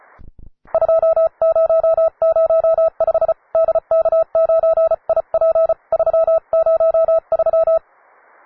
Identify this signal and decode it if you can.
Morse code, heard as a keyed tone, about 34 wpm, 650 Hz. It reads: 2005 DK9IP 303